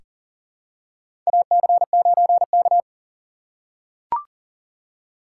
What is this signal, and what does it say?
Morse code, heard as a keyed tone, about 40 wpm, 700 Hz. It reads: AC9K
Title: USA Call Signs - 3x Character Spacing - 40wpm